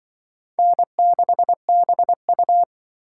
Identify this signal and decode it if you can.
Morse code, heard as a keyed tone, about 25 wpm, 700 Hz. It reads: N6BU
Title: Speed Racing - USA 1x2 Call Signs 20wpm